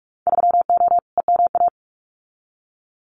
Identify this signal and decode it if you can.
Morse code, heard as a keyed tone, about 46 wpm, 700 Hz. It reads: 3O EMA
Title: Winter Field Day Exchanges 45wpm